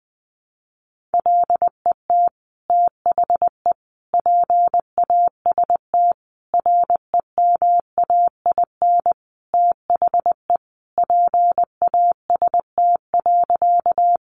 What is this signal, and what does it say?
Morse code, heard as a keyed tone, about 20 wpm, 700 Hz. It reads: LET THE PAST REMAIN THE PAST.
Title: Sentences from Top 1200 Words 20wpm